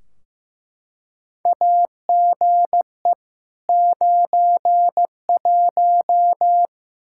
Morse code, heard as a keyed tone, about 15 wpm, 700 Hz. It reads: AGE 91
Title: Practice QSO - Exchange 3 - More Info - 15wpm